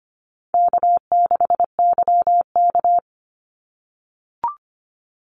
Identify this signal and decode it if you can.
Morse code, heard as a keyed tone, about 25 wpm, 700 Hz. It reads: K6YK
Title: NA SOTA Chaser Callsigns 25wpm